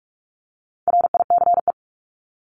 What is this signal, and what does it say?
Morse code, heard as a keyed tone, about 44 wpm, 700 Hz. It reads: RICE
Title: Top 1501-1750 Words 45wpm